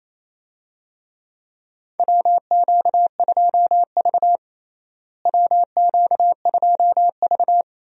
Morse code, heard as a keyed tone, about 28 wpm, 700 Hz. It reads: WQ2V WQ2V
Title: USA Call Signs - 4 Characters - Repeated 2x - 28wpm